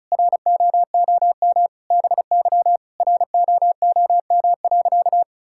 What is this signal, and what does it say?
Morse code, heard as a keyed tone, about 36 wpm, 700 Hz. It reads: ROOM BY ROOM.
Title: Sentences from Top 1000 Words - Review Part 5 - 35wpm